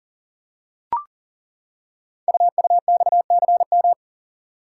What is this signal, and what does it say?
Morse code, heard as a keyed tone, about 38 wpm, 700 Hz. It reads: UUXCM
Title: Code Groups - 5 Letters - 40wpm